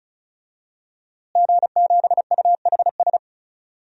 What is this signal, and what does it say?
Morse code, heard as a keyed tone, about 34 wpm, 700 Hz. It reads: G7UHS
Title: DXCC-100 Call Signs - 5 Characters - Repeated 3x - 35wpm